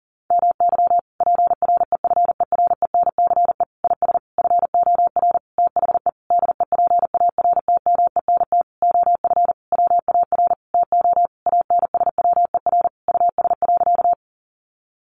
Morse code, heard as a keyed tone, about 40 wpm, 700 Hz. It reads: MY PREFERENCE IS FOR THE DEPARTMENT OF WAR TO ANSWER US.
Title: Sentences from Top 1500 Words 40wpm